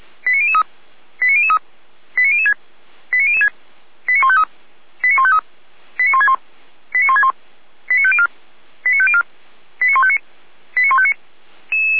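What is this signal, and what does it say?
Five-tone selective call (ZVEI), heard as quick runs of five tones, tones of about 70 ms. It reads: 89003 89003 89006 89006 89153 89153 89161 89161 89504 89504 89159 89159
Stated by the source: ZVEI five-tone call received by SDR, hosted by feuerwehr-orlamuende (fire brigade Orlamünde).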